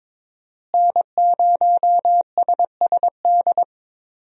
Morse code, heard as a keyed tone, about 22 wpm, 700 Hz. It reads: N0SSD
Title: USA Call Signs - 5 Characters - Repeated 3x - 22wpm